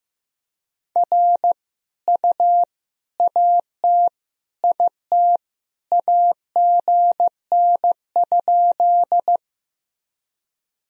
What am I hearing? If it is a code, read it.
Morse code, heard as a keyed tone, about 15 wpm, 700 Hz. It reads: R U AT IT AGN?
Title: Sentences from Top 200 Words and Abbreviations 15wpm